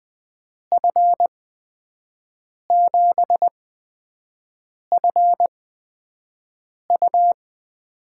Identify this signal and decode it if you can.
Morse code, heard as a keyed tone, about 20 wpm, spaced as for 5 wpm, 700 Hz. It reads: F7FU
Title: USA Call Signs - 6 Characters - 8x Character Spacing - 20wpm